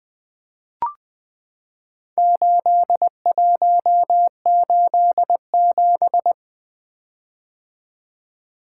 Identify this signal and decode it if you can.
Morse code, heard as a keyed tone, about 20 wpm, 700 Hz. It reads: 8187
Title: Four Letter-Number Combinations - 20wpm